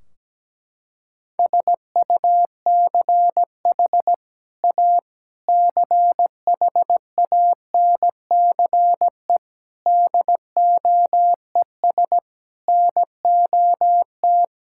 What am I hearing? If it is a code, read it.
Morse code, heard as a keyed tone, about 17 wpm, 700 Hz. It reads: SUCH A CHANCE DOES NOT
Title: Sentences from Top 1600 Words 17wpm